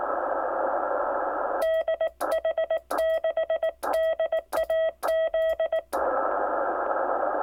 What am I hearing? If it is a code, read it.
Morse code, heard as a keyed tone, about 19 wpm, 1950 Hz, with noise. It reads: DH6DAZ
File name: ortsverbaende / Aurora-QSO 6m pa5wt